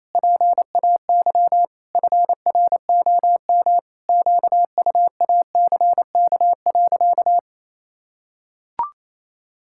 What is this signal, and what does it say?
Morse code, heard as a keyed tone, about 28 wpm, 700 Hz. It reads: PAY FROM QUACK.
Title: Pangram Challenge v2 - 28wpm